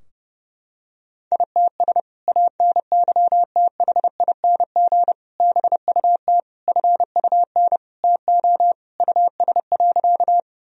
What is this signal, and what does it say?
Morse code, heard as a keyed tone, about 29 wpm, 700 Hz. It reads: ITS ANYTHING BUT FUN TO US.
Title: Sentences from Top 1000 Words - Review Part 4 - 30wpm